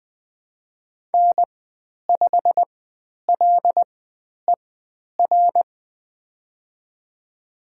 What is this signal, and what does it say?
Morse code, heard as a keyed tone, about 20 wpm, spaced as for 10 wpm, 700 Hz. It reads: N5LER
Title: 109 Lesson 18g - Letter m - Call Signs 20wpm